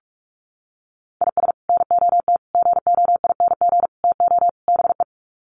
Morse code, heard as a keyed tone, about 44 wpm, 700 Hz. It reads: IS NOT GOING TO BE
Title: Sets of 5 Words Encore2 45wpm